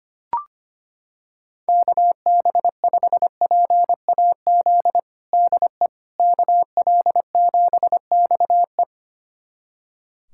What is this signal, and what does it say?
Morse code, heard as a keyed tone, about 25 wpm, 700 Hz. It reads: KB5PAZ DE KL7XE
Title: USA Callsign DE Callsign 25wpm